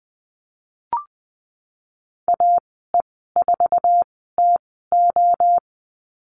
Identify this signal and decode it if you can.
Morse code, heard as a keyed tone, about 20 wpm, spaced as for 15 wpm, 700 Hz. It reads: AE4TO
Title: 048 Lesson 9g - Number 4 - Call Signs 20wpm